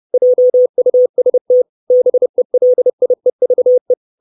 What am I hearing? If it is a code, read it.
Morse code, heard as a keyed tone, about 30 wpm, 500 Hz. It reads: JUST BELIEVE
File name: 632 words / 632 Words 30WPM